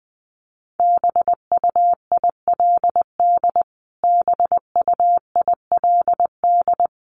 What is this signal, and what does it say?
Morse code, heard as a keyed tone, about 20 wpm, 700 Hz. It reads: BUILD BUILD
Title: Top 300 Words Repeated 20wpm